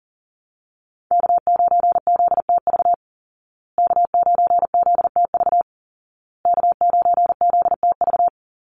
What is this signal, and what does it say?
Morse code, heard as a keyed tone, about 40 wpm, 700 Hz. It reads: K9ZTV K9ZTV K9ZTV
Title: Call Signs Repeated 2x - 40wpm